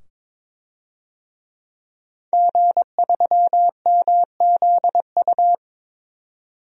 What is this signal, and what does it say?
Morse code, heard as a keyed tone, about 22 wpm, 700 Hz. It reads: G3MZU
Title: DXCC-100 Call Signs - 5 Characters - 22wpm